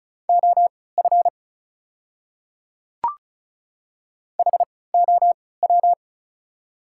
Morse code, heard as a keyed tone, about 34 wpm, spaced as for 20 wpm, 690 Hz. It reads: OF HOW
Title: 3x Character Spacing - Top 100 Words - 35wpm